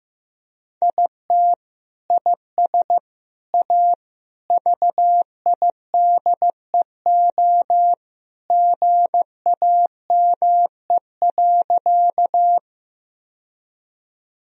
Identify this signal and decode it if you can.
Morse code, heard as a keyed tone, about 15 wpm, 700 Hz. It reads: IT IS A VIDEO GAME.
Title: Sentences from Top 1000 Words - Review Part 4 - 15wpm